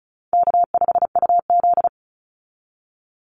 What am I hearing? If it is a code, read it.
Morse code, heard as a keyed tone, about 34 wpm, 700 Hz. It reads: K5UZ
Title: Call Signs - 35wpm